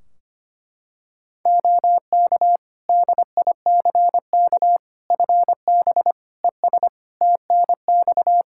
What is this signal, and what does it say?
Morse code, heard as a keyed tone, about 25 wpm, 700 Hz. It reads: OK DICK FB ES TNX
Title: Practice QSO - Exchange 2 - Info - 25wpm